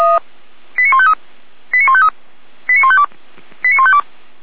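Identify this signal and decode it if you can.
Five-tone selective call (ZVEI), heard as quick runs of five tones, tones of about 70 ms. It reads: 89153 89153 89152 89152
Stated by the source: ZVEI five-tone selective call received by SDR.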